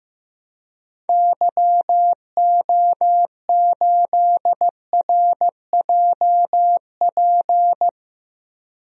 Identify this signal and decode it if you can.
Morse code, heard as a keyed tone, about 15 wpm, 700 Hz. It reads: YO8RJP
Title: DXCC-100 Call Signs - 6 Characters - 15wpm